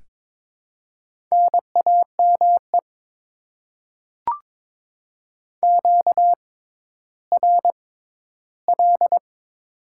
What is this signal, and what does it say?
Morse code, heard as a keyed tone, about 22 wpm, spaced as for 13 wpm, 690 Hz. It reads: NAME QRL
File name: Top 20 QSO Elements - 6x Character Spacing - 22wpm